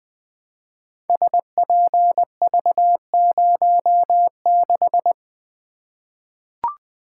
Morse code, heard as a keyed tone, about 20 wpm, 700 Hz. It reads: SPV06
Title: Code Groups - 5 Letter:Numbers - 20wpm